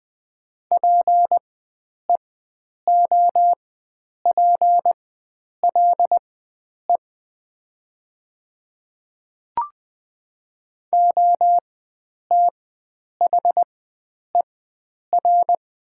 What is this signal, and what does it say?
Morse code, heard as a keyed tone, about 20 wpm, spaced as for 9 wpm, 700 Hz. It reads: PEOPLE OTHER
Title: 4x Character Spacing - Top 100 Words - 20wpm